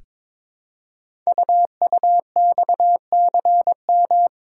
Morse code, heard as a keyed tone, about 22 wpm, 700 Hz. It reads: UUXCM
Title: Code Groups - 5 Letters - 22wpm